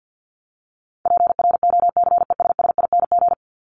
Morse code, heard as a keyed tone, about 48 wpm, 700 Hz. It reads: PROCESSING
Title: ING Suffix - Top 100 Words with Base Words 50wpm